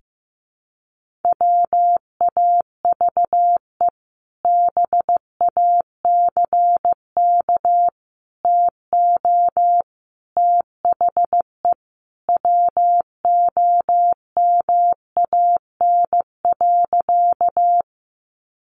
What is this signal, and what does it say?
Morse code, heard as a keyed tone, about 15 wpm, 700 Hz. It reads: WAVE BACK TO THE WOMAN.
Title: Sentences from Top 1400 Words 15wpm